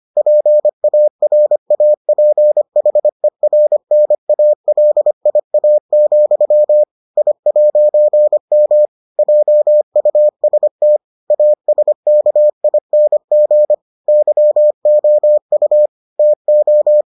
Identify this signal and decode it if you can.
Morse code, heard as a keyed tone, about 25 wpm, 600 Hz. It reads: PARAPHERNALIA, I'M JUST ASKING YOU TO